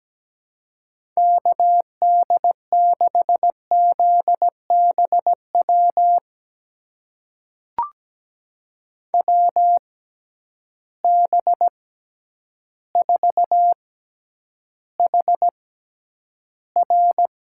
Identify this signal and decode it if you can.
Morse code, heard as a keyed tone, about 17 wpm, spaced as for 9 wpm, 700 Hz. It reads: KD6ZBW WB4HR
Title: USA Call Signs - 6 Characters - 6x Character Spacing - 17wpm